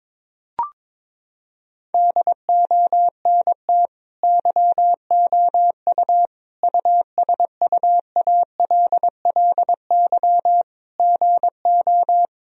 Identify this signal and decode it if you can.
Morse code, heard as a keyed tone, about 22 wpm, 700 Hz. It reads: DONT YOU USUALLY GO
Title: Sentences from Top 1000 Words - Review Part 3 - 22wpm